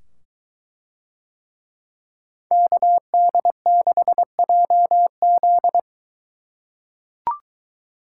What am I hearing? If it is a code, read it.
Morse code, heard as a keyed tone, about 22 wpm, 700 Hz. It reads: KD6JZ